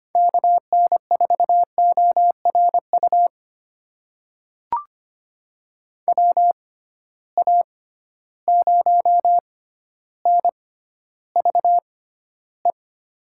Morse code, heard as a keyed tone, about 25 wpm, spaced as for 13 wpm, 700 Hz. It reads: KN4ORU WA0NVE
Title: USA Call Signs - 6 Characters - 6x Character Spacing - 25wpm